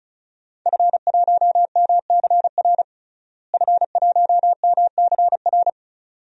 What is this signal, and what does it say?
Morse code, heard as a keyed tone, about 36 wpm, 700 Hz. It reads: F1MCR F1MCR
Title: DXCC-100 Call Signs - 5 Characters - Repeated 2x - 35wpm